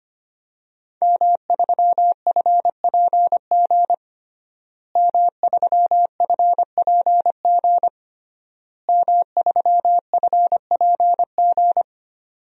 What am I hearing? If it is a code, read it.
Morse code, heard as a keyed tone, about 25 wpm, 700 Hz. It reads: M3FPG M3FPG M3FPG
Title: DXCC-100 Call Signs - 5 Characters - Repeated 3x - 25wpm